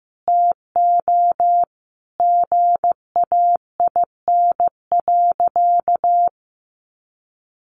Morse code, heard as a keyed tone, about 15 wpm, 700 Hz. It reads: TO GAIN.
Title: Sentences from Top 1100 Words 15wpm